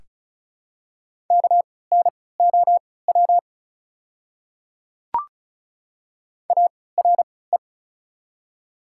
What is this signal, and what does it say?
Morse code, heard as a keyed tone, about 34 wpm, spaced as for 20 wpm, 700 Hz. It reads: KNOW ARE
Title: 3x Character Spacing - Top 200 Words - 35wpm